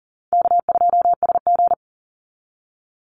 Morse code, heard as a keyed tone, about 40 wpm, 700 Hz. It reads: K2SG
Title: Call Signs Repeated 2x - 40wpm